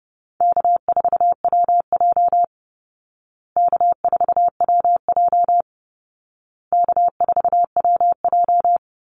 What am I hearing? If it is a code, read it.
Morse code, heard as a keyed tone, about 30 wpm, 700 Hz. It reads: K4WJ K4WJ K4WJ
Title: Call Signs Repeated 2x - 30wpm